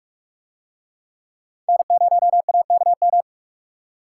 Morse code, heard as a keyed tone, about 44 wpm, 700 Hz. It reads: N0AKM